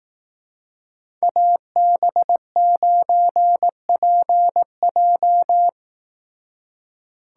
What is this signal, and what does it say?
Morse code, heard as a keyed tone, about 18 wpm, 700 Hz. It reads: AB9PJ
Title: Speed Racing - USA 2x2 Call Signs 15wpm